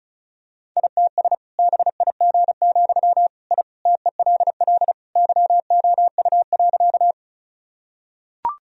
Morse code, heard as a keyed tone, about 36 wpm, 700 Hz. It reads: ITS BIG, I TELL YOU.
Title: Sentences from Top 1000 Words - Review Part 1 - 35wpm